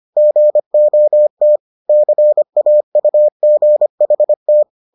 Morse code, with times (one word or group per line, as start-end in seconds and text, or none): 0.2-1.6 GOT
1.9-4.6 CAUGHT